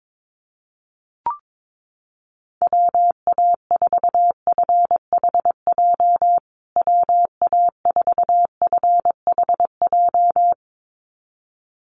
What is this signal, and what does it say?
Morse code, heard as a keyed tone, about 22 wpm, 700 Hz. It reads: WA4FHJ WA4FHJ
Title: USA Call Signs Repeated 2x 22wpm